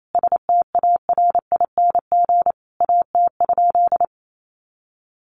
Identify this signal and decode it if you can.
Morse code, heard as a keyed tone, about 28 wpm, 700 Hz. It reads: STARING AT?
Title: A Sound Of Thunder 28wpm